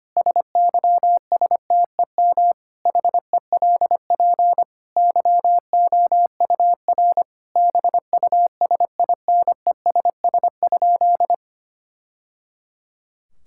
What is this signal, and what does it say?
Morse code, heard as a keyed tone, about 25 wpm, 700 Hz. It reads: SYSTEM HELP YOUR BUSINESS?
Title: Sentences from Top 1000 Words - Review Part 5 - 25wpm